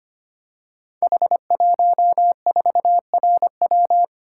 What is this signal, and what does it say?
Morse code, heard as a keyed tone, about 25 wpm, 700 Hz. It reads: H14RW
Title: Code Groups - 5 Letter:Numbers - 25wpm